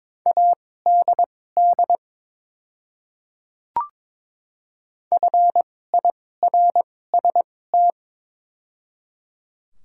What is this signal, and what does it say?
Morse code, heard as a keyed tone, about 22 wpm, spaced as for 16 wpm, 700 Hz. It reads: ADD FIRST
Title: 2x Character Spacing - Top 200 Words - 22wpm